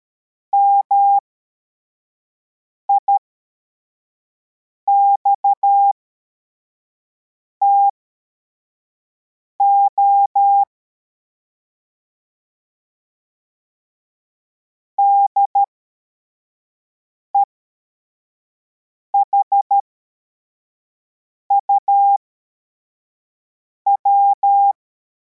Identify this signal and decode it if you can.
Morse code, heard as a keyed tone, about 13 wpm, spaced as for 4 wpm, 800 Hz. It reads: MIXTO DEHUW
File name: formation / cw / Text71